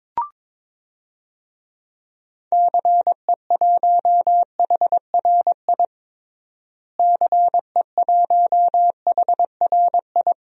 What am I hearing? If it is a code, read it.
Morse code, heard as a keyed tone, about 22 wpm, 700 Hz. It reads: CE1HRI CE1HRI
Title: DXCC-100 Call Signs - 6 Characters - Repeated 2x - 22wpm